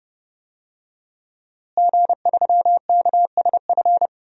Morse code, heard as a keyed tone, about 30 wpm, 700 Hz. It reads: G3KSF
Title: DXCC-100 Call Signs - 5 Characters - Repeated 3x - 30wpm